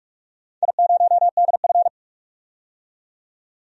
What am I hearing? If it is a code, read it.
Morse code, heard as a keyed tone, about 44 wpm, 700 Hz. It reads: I0DF
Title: DXCC-100 Call Signs - 4 Characters - 45wpm